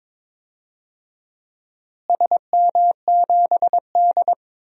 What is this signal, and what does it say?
Morse code, heard as a keyed tone, about 22 wpm, 700 Hz. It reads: SM7D
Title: DXCC-100 Call Signs - 4 Characters - Repeated 2x - 22wpm